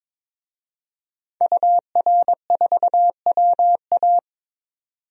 Morse code, heard as a keyed tone, about 21 wpm, 700 Hz. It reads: UR4WA
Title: DXCC-100 Call Signs - 5 Characters - 22wpm